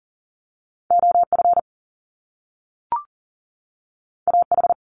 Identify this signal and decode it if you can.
Morse code, heard as a keyed tone, about 40 wpm, 700 Hz. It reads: OF AH
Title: Top Two Letter Combinations in Words 40wpm